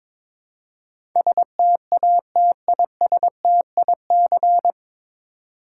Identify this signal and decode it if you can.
Morse code, heard as a keyed tone, about 22 wpm, 700 Hz. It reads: STATISTIC
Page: Morse Code Ninja practice 22wpm